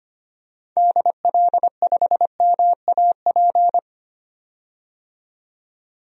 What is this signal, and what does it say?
Morse code, heard as a keyed tone, about 25 wpm, 700 Hz. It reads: DL5MAP
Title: DXCC-100 Call Signs - 6 Characters - 25wpm